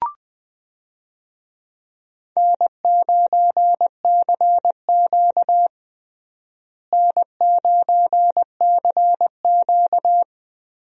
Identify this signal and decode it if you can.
Morse code, heard as a keyed tone, about 20 wpm, 700 Hz. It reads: N9CQ N9CQ
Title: USA Call Signs - 4 Characters - Repeated 2x - 20wpm